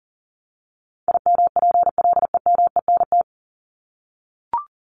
Morse code, heard as a keyed tone, about 40 wpm, 700 Hz. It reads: IMPLEMENT
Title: Top 1501-1750 Words 40wpm